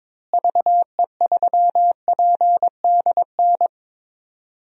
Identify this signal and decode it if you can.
Morse code, heard as a keyed tone, about 22 wpm, 700 Hz. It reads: VE3PDN
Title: DXCC-100 Call Signs - 6 Characters - Repeated 3x - 22wpm